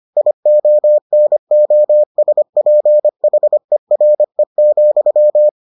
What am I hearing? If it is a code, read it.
Morse code, heard as a keyed tone, about 25 wpm, 600 Hz. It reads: IONOSPHERE,